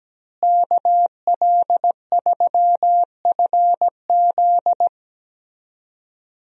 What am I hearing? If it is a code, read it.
Morse code, heard as a keyed tone, about 17 wpm, 700 Hz. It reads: KL3FZ
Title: USA Call Signs - 5 Characters - Repeated 3x - 17wpm